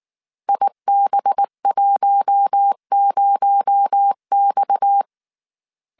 Morse code, heard as a keyed tone, about 19 wpm, 800 Hz. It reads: IB10X
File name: CW / DET / 1 6 1500C S A 18wpm 2255